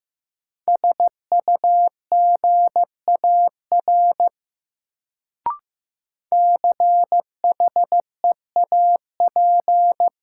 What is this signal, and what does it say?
Morse code, heard as a keyed tone, about 16 wpm, 700 Hz. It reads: SUGAR CHEAP
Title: Sentences from Top 1200 Words - New Word Drill Alt - 15wpm